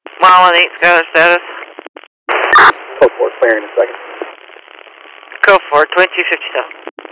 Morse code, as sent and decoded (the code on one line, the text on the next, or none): none